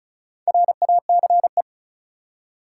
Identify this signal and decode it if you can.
Morse code, heard as a keyed tone, about 36 wpm, 700 Hz. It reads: RACE
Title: Four Letter Words - 35wpm